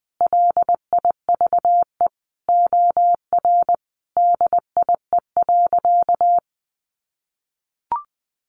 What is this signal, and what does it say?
Morse code, heard as a keyed tone, about 20 wpm, 700 Hz. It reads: LIVE OR DIE.
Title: Sentences from Top 1300 Words 20wpm